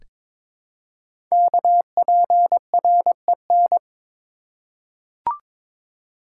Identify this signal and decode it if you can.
Morse code, heard as a keyed tone, about 23 wpm, 700 Hz. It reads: KPREN
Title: Code Groups - 5 Letters - 22wpm